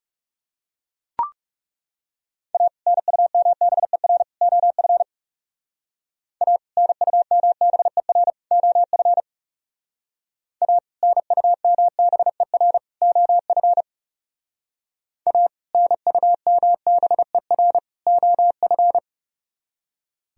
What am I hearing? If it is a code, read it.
Morse code, heard as a keyed tone, about 38 wpm, 700 Hz. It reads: A NUMBER OF A NUMBER OF A NUMBER OF A NUMBER OF